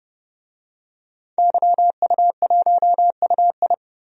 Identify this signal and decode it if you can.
Morse code, heard as a keyed tone, about 30 wpm, 700 Hz. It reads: YU1UI
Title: DXCC-100 Call Signs - 5 Characters - 30wpm